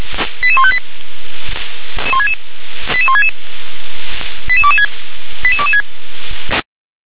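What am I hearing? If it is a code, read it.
Five-tone selective call (ZVEI), heard as quick runs of five tones, tones of about 70 ms. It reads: 88158 88166 88226 88226
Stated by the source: ZVEI five-tone call 88126, received by SDR.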